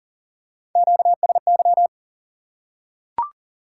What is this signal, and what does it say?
Morse code, heard as a keyed tone, about 43 wpm, 700 Hz. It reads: QSY
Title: Top 180 QSO Elements 40wpm